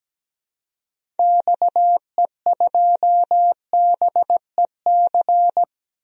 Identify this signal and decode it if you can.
Morse code, heard as a keyed tone, about 17 wpm, 700 Hz. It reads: XE2BEC
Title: DXCC-100 Call Signs - 6 Characters - 17wpm